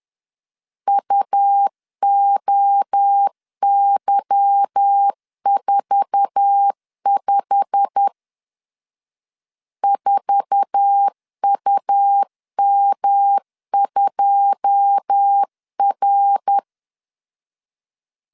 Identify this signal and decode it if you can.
Morse code, heard as a keyed tone, about 10 wpm, 800 Hz. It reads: UOY45 4UM2R
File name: LEZIONI / 9 2 10wpm 2736